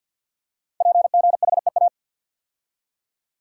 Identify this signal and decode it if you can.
Morse code, heard as a keyed tone, about 53 wpm, 700 Hz. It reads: PGHEA
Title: Code Groups - 5 Letters - 50wpm